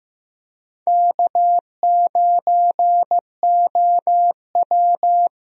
Morse code, heard as a keyed tone, about 15 wpm, 700 Hz. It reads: K9OW
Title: USA Call Signs - 4 Characters - Repeated 3x - 15wpm